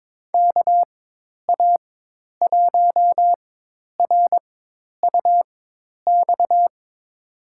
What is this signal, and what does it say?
Morse code, heard as a keyed tone, about 22 wpm, spaced as for 10 wpm, 700 Hz. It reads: KA1RUX
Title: USA Call Signs - 4x Character Spacing - 22wpm